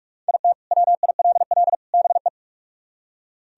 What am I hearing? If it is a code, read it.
Morse code, heard as a keyed tone, about 62 wpm, spaced as for 53 wpm, 700 Hz. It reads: IT WILL BE
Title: Speed-Racing - Sets of 3 Words 45wpm